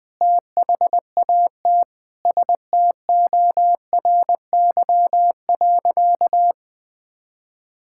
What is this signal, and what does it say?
Morse code, heard as a keyed tone, about 21 wpm, 700 Hz. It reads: THAT STORY.
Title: Sentences with 5-Word Phrases - 20wpm